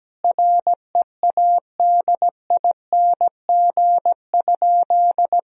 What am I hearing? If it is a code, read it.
Morse code, heard as a keyed tone, about 17 wpm, 700 Hz. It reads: READING?
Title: Sentences from Top 1000 Words - Review Part 3 - 17wpm